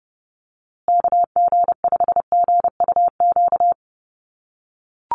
Morse code, heard as a keyed tone, about 30 wpm, 700 Hz. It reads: KG5GUQ
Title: USA Call Signs Repeated 3x 30wpm